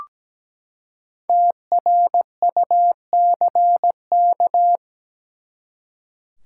Morse code, heard as a keyed tone, about 17 wpm, 700 Hz. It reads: TRUCK
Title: Five Letter Words - 17wpm